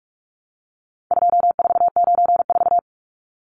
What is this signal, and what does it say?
Morse code, heard as a keyed tone, about 44 wpm, 700 Hz. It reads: 2494
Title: Numbers - 4-Digit 45wpm